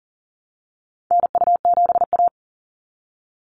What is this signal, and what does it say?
Morse code, heard as a keyed tone, about 41 wpm, 700 Hz. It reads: NU7A
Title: NA SOTA Activator Callsigns 40wpm